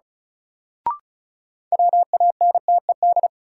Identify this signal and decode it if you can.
Morse code, heard as a keyed tone, about 36 wpm, 700 Hz. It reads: WANTED
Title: Sets of 2 Words - New Word Drill - 35wpm